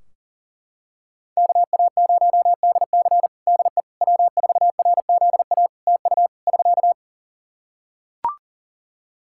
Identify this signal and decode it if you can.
Morse code, heard as a keyed tone, about 38 wpm, 700 Hz. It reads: KA0DC DE W4RZA TU <SK>
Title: Practice QSO - Exchange 4 - Ending - 40wpm